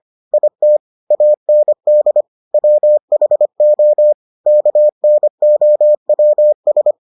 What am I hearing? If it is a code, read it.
Morse code, heard as a keyed tone, about 25 wpm, 600 Hz. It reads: IT AND WHO KNOWS